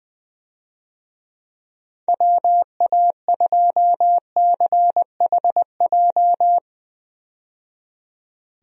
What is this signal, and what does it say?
Morse code, heard as a keyed tone, about 21 wpm, 700 Hz. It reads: WA2CHJ
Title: USA Call Signs - 6 Characters - 20wpm